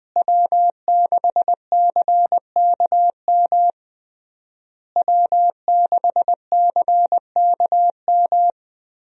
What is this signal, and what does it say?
Morse code, heard as a keyed tone, about 20 wpm, 700 Hz. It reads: W6CKM W6CKM
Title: USA Call Signs - 5 Characters - Repeated 3x - 20wpm